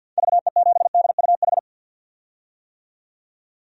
Morse code, heard as a keyed tone, about 46 wpm, 700 Hz. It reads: VE7DUH
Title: DXCC-100 Call Signs - 6 Characters - 50wpm